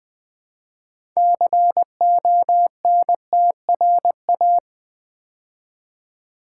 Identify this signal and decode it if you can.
Morse code, heard as a keyed tone, about 20 wpm, 700 Hz. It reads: CONTRA
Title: Build A Word - 2 Letters At A Time - 8-Letter Words - 20wpm